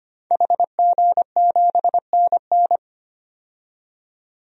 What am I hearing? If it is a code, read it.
Morse code, heard as a keyed tone, about 25 wpm, 700 Hz. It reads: HG7NN
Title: DXCC-100 Call Signs - 5 Characters - 25wpm